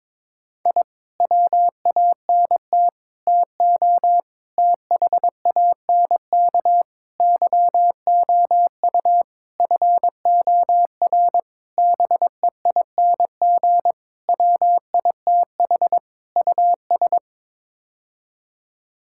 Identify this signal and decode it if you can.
Morse code, heard as a keyed tone, about 22 wpm, 700 Hz. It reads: I WANT TO THANK YOU FOR BEING WITH US
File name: Sets of 9 Words - 22wpm